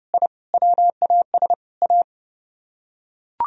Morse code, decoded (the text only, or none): I WAS A